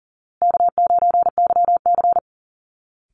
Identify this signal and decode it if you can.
Morse code, heard as a keyed tone, about 37 wpm, 700 Hz. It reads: K9YC
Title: Easy Call Signs Repeated 3x - 40wpm